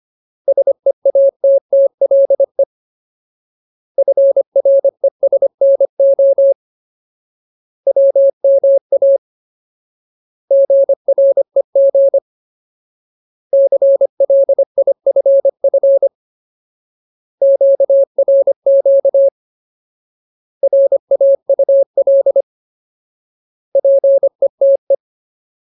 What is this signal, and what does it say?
Morse code, heard as a keyed tone, about 25 wpm, 550 Hz. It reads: SEATTLE FRESNO WMA GREG CLIFF QRQ RAUL PETE
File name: Ultimate Word List / 25WPM Ultimate